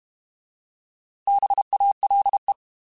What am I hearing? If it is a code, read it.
Morse code, heard as a keyed tone, about 32 wpm, 800 Hz. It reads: BALE
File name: Alphabetical / Alphabetical mp3s / Bs @30